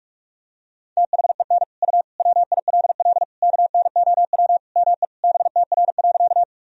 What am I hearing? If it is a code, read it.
Morse code, heard as a keyed tone, about 41 wpm, 700 Hz. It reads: THEN U WILL KNOW ME BTR.
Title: Sentences from Top 400 Words and Abbreviations 45wpm